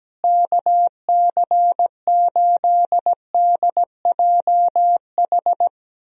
Morse code, heard as a keyed tone, about 17 wpm, 700 Hz. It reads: KC8DJH
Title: USA Call Signs - 6 Characters - Repeated 3x - 17wpm